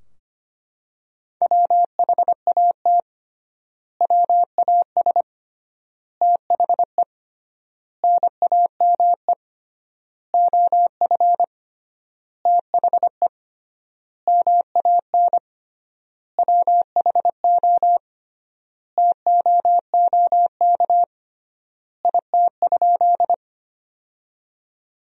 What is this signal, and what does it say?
Morse code, heard as a keyed tone, about 25 wpm, 700 Hz. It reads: WHAT WAS THE NAME OF THE MAN WHO TOOK IT?
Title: 3x Word-Spacing - Sentences from Top 300 Words - 25wpm